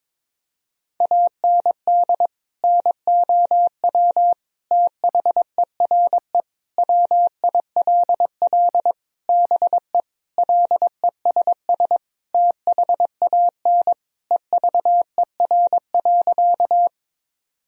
Morse code, heard as a keyed tone, about 21 wpm, 700 Hz. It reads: AND NOW THERE WILL BE LESS THAN EVER.
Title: Sentences from Top 1000 Words - Review Part 2 - 22wpm